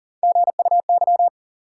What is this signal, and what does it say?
Morse code, heard as a keyed tone, about 40 wpm, 700 Hz. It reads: GUY
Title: Three Letter Words - 40wpm